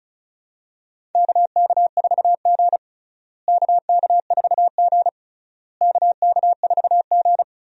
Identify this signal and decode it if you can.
Morse code, heard as a keyed tone, about 36 wpm, 700 Hz. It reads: KK4G KK4G KK4G
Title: USA Call Signs - 4 Characters - Repeated 3x - 35wpm